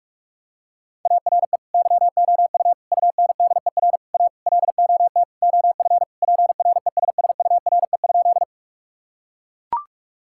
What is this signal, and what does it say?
Morse code, heard as a keyed tone, about 41 wpm, 700 Hz. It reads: ARE YOU UNDER A LOT OF PRESSURE?
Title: Sentences from Top 1000 Words - Review Part 6 - 45wpm